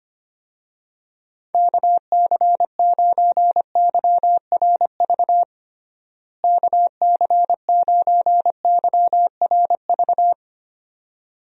Morse code, heard as a keyed tone, about 25 wpm, 700 Hz. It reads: KC9YRV KC9YRV
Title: USA Call Signs - 6 Characters - Repeated 2x - 25wpm